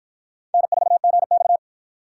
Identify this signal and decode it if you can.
Morse code, heard as a keyed tone, about 53 wpm, 700 Hz. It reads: N4GX